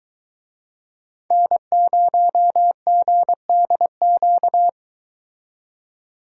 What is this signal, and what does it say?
Morse code, heard as a keyed tone, about 23 wpm, 700 Hz. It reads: N0GDQ